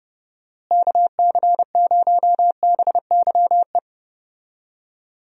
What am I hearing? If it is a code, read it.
Morse code, heard as a keyed tone, about 30 wpm, 700 Hz. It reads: KC0BYE
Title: USA Call Signs - 6 Characters - Repeated 2x - 30wpm